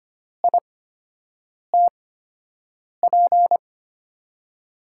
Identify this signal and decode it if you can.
Morse code, heard as a keyed tone, about 25 wpm, spaced as for 7 wpm, 700 Hz. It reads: ITP
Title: USA Call Signs - 8x Character Spacing - 25wpm